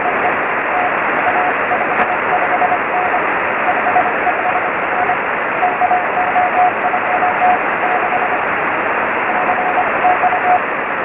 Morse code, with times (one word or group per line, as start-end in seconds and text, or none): none